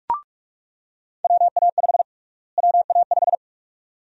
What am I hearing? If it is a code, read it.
Morse code, heard as a keyed tone, about 41 wpm, 700 Hz. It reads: WA5 WA5
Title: USA Call Sign Prefixes - Repeated 2x - 45wpm